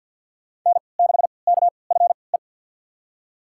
Morse code, heard as a keyed tone, about 53 wpm, spaced as for 29 wpm, 700 Hz. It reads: N6KFE
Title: USA Call Signs - 5 Characters - 3x Character Spacing - 50wpm